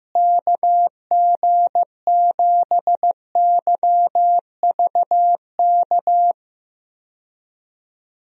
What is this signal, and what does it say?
Morse code, heard as a keyed tone, about 15 wpm, 700 Hz. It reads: KG7YVK
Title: USA Call Signs - 6 Characters - Repeated 2x - 15wpm